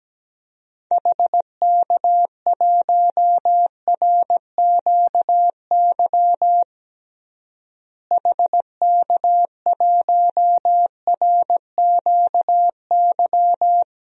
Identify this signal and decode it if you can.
Morse code, heard as a keyed tone, about 17 wpm, 700 Hz. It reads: HK1RQY HK1RQY
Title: DXCC-100 Call Signs - 6 Characters - Repeated 2x - 17wpm